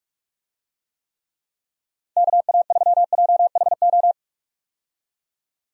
Morse code, heard as a keyed tone, about 44 wpm, 700 Hz. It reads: KA3JHO
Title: USA Call Signs - 6 Characters - 45wpm